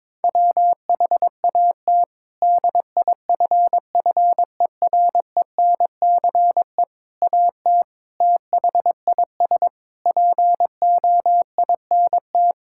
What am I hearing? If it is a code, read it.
Morse code, heard as a keyed tone, about 22 wpm, 700 Hz. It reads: WHAT DIFFERENCE AT THIS POINT
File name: practice / 22wpm / Sets of 8 Words - 22wpm